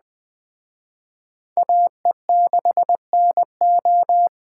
Morse code, heard as a keyed tone, about 20 wpm, 700 Hz. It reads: AE6NO
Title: Speed Racing - USA 2x2 Call Signs 17wpm